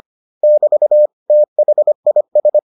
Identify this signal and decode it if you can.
Morse code, heard as a keyed tone, about 25 wpm, 600 Hz. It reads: <BT> THIS